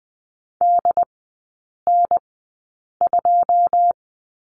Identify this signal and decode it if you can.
Morse code, heard as a keyed tone, about 20 wpm, spaced as for 8 wpm, 700 Hz. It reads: DN2
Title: 107 Lesson 18e - Letter m - Three Characters 20wpm